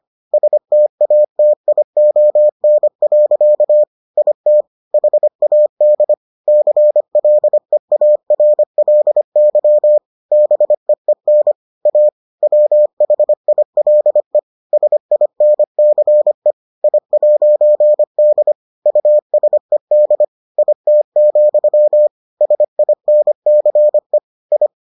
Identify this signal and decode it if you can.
Morse code, heard as a keyed tone, about 25 wpm, 600 Hz. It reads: STATION. IT HAD CLEARLY BEEN A WHILE SINCE I'D USED IT, SINCE I